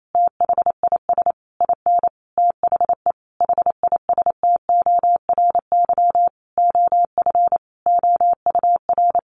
Morse code, heard as a keyed tone, about 28 wpm, 700 Hz. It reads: THIS IN THE HISTORY OF OUR
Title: Sentences from Top 1500 Words 28wpm